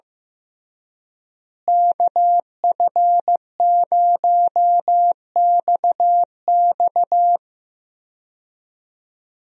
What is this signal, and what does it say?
Morse code, heard as a keyed tone, about 15 wpm, 700 Hz. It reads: KF0XX
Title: USA Call Signs - 5 Characters - Repeated 2x - 15wpm